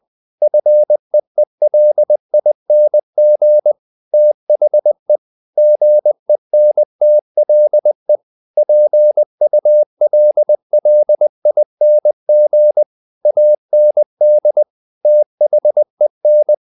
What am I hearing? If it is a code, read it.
Morse code, heard as a keyed tone, about 20 wpm, 600 Hz. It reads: FEELING THE GENTLE PULLING AND THEN
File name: the old man and the sea / the old man and the sea part 1 20wpm